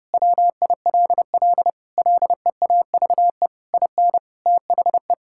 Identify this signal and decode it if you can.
Morse code, heard as a keyed tone, about 30 wpm, 700 Hz. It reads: WILL LEAVE IN THE
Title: Sentences from Top 1000 Words - Review Part 6 - 30wpm